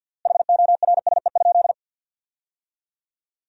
Morse code, heard as a keyed tone, about 53 wpm, 700 Hz. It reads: HORSE?
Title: 3x Word-Spacing - Sentences from Top 500 Words - 50wpm